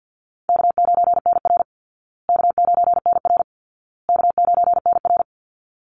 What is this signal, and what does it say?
Morse code, heard as a keyed tone, about 48 wpm, 700 Hz. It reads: K9NR K9NR K9NR
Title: Easy Call Signs Repeated 3x - 50wpm